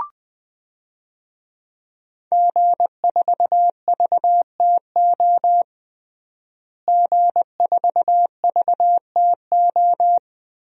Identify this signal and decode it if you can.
Morse code, heard as a keyed tone, about 21 wpm, 700 Hz. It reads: G4VTO G4VTO
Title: DXCC-100 Call Signs - 5 Characters - Repeated 3x - 20wpm